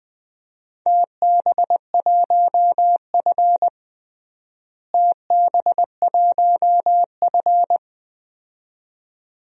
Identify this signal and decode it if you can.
Morse code, heard as a keyed tone, about 20 wpm, 700 Hz. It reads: TB1F TB1F
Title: DXCC-100 Call Signs - 4 Characters - Repeated 2x - 20wpm